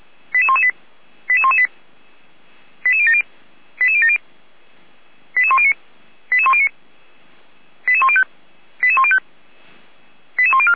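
Five-tone selective call (ZVEI), heard as quick runs of five tones, tones of about 70 ms. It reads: 88118 88118 88070 88070 88109 88109 88105 88105 88104
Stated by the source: ZVEI five-tone selective call received on an SDR, archived by Feuerwehr Orlamünde.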